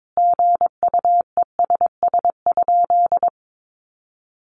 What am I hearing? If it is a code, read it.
Morse code, heard as a keyed tone, about 22 wpm, 700 Hz. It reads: GUESS?
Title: Sentences from Top 1500 Words 22wpm